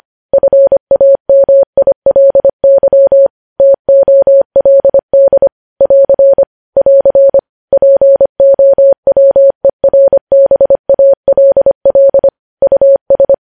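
Morse code, heard as a keyed tone, about 25 wpm, 550 Hz. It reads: FAMILY TOLD <AR> <AR> POWERBALL US
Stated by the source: Morse code news headlines sent at 25 wpm.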